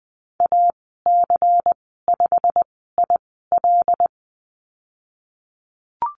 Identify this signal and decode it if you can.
Morse code, heard as a keyed tone, about 20 wpm, spaced as for 15 wpm, 700 Hz. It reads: AC5IL
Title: 147 Lesson 24g - Letter y - Call Signs 20wpm